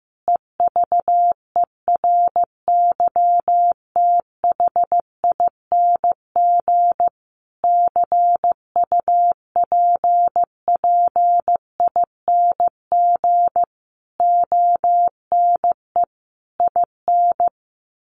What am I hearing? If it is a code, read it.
Morse code, heard as a keyed tone, about 15 wpm, 700 Hz. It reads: EVERYTHING CUPPING ONE IN
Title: A Sound Of Thunder 15wpm